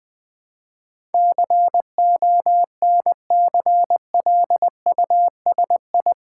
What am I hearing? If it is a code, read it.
Morse code, heard as a keyed tone, about 20 wpm, 700 Hz. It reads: CONCLUSI
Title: Build A Word - 4 Letters At A Time - 10-Letter Words - 20wpm